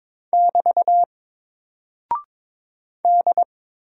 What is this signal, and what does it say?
Morse code, heard as a keyed tone, about 22 wpm, 700 Hz. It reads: <BT> D
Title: Single Character (Letter:Number:Punctuation:Prosign) 22wpm